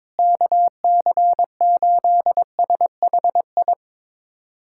Morse code, heard as a keyed tone, about 21 wpm, 700 Hz. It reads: KC8SHI
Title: USA Callsign DE Callsign - 6 Characters - 22wpm